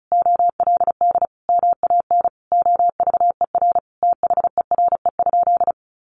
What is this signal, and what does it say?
Morse code, heard as a keyed tone, about 34 wpm, 700 Hz. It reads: OLD MAN OVER THERE?
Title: Sentences from Top 200 Words 35wpm